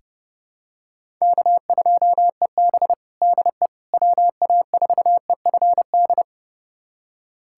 Morse code, heard as a keyed tone, about 30 wpm, 700 Hz. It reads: K2EB DE WA4EFD
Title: USA Callsign DE Callsign 30wpm